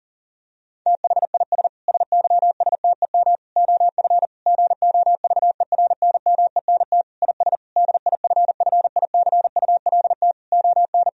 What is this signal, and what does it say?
Morse code, heard as a keyed tone, about 40 wpm, 700 Hz. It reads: THIS SYSTEM OF GOVERNMENT IS DIFFICULT ON
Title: Sentences from Top 1000 Words - Review Part 3 - 40wpm